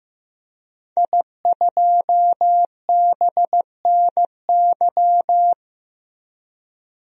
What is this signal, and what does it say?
Morse code, heard as a keyed tone, about 15 wpm, 700 Hz. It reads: I2BNY
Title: DXCC-100 Call Signs - 5 Characters - Repeated 3x - 15wpm